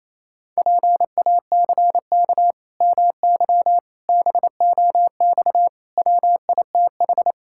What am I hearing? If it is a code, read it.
Morse code, heard as a keyed tone, about 27 wpm, 700 Hz. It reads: PACK MY BOX WITH
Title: Pangram Challenge v2 - 28wpm